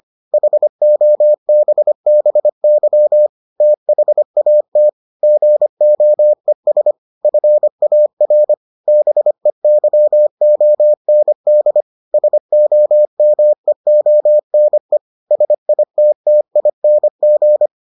Morse code, heard as a keyed tone, about 25 wpm, 600 Hz. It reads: HOBBY THAT GOES FAR BEYOND SOMEONE SITTING